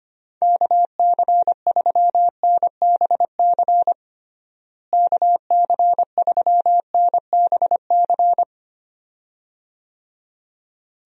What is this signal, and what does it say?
Morse code, heard as a keyed tone, about 25 wpm, 700 Hz. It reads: KC3NBC KC3NBC
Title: USA Call Signs - 6 Characters - Repeated 2x - 25wpm